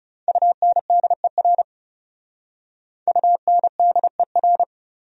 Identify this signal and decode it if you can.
Morse code, heard as a keyed tone, about 38 wpm, 700 Hz. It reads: UNDER UNDER
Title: Speed-Racing Top 300 Words 30wpm